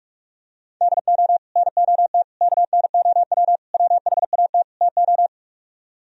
Morse code, heard as a keyed tone, about 47 wpm, 700 Hz. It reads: DO NOT KNOW WHAT TO